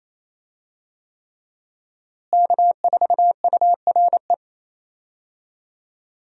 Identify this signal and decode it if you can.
Morse code, heard as a keyed tone, about 28 wpm, 700 Hz. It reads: K4URE
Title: USA Call Signs - 5 Characters - Repeated 3x - 28wpm